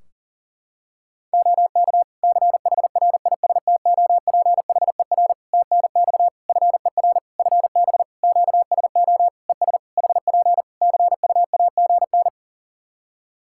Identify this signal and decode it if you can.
Morse code, heard as a keyed tone, about 40 wpm, 700 Hz. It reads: OK CHRISTOPHER TNX FER FB QSO ES HP CUAGN
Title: Practice QSO - Exchange 4 - Ending - 40wpm